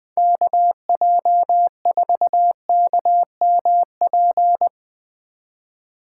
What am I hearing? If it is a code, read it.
Morse code, heard as a keyed tone, about 20 wpm, 700 Hz. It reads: KJ4KMP
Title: Speed Racing - USA 2x3 Call Signs 17wpm